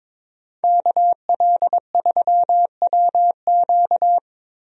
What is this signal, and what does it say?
Morse code, heard as a keyed tone, about 22 wpm, 700 Hz. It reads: KL3WQ
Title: USA Call Signs - 5 Characters - Repeated 3x - 22wpm